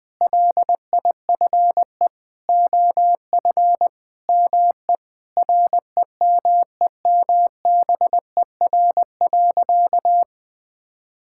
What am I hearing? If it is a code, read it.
Morse code, heard as a keyed tone, about 21 wpm, 700 Hz. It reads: LIFE OF ME REMEMBER.
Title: Sentences with 8-Word Phrases - 20wpm